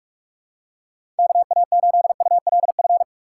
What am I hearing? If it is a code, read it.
Morse code, heard as a keyed tone, about 46 wpm, 700 Hz. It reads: KA8ULF
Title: USA Call Signs - 6x Character Spacing - 45wpm